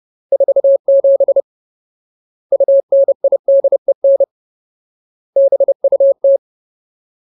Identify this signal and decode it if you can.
Morse code, heard as a keyed tone, about 30 wpm, 550 Hz. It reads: 47 UNIDEN BUT
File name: Ultimate Word List / 30WPM Ultimate